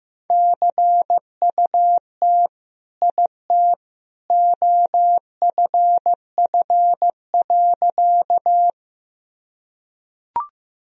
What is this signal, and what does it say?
Morse code, heard as a keyed tone, about 15 wpm, 700 Hz. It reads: CUT IT OFF.
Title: Sentences from Top 1000 Words - Review Part 1 - 15wpm